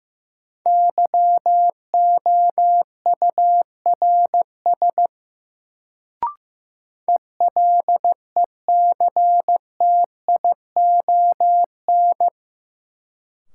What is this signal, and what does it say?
Morse code, heard as a keyed tone, about 15 wpm, 700 Hz. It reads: YOURS ELECTION
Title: Sentences from Top 1500 Words - New Word Drill Alt - 15wpm